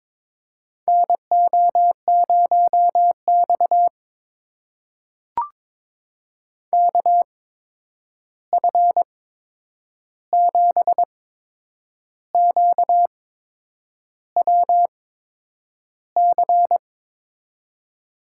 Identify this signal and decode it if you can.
Morse code, heard as a keyed tone, about 22 wpm, spaced as for 8 wpm, 700 Hz. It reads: NO0X KF7QWC
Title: USA Call Signs - 8x Character Spacing - 22wpm